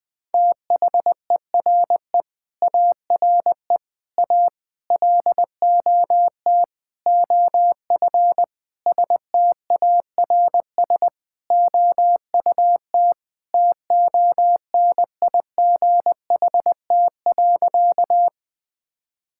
Morse code, heard as a keyed tone, about 20 wpm, 700 Hz. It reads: THERE ARE A LOT OF STARS OUT TONIGHT.
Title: Sentences with 5-Word Phrases - 20wpm